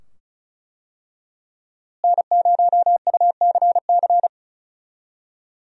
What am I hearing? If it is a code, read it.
Morse code, heard as a keyed tone, about 34 wpm, 700 Hz. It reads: N0UCC
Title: USA Call Signs - 5 Characters - Repeated 2x - 35wpm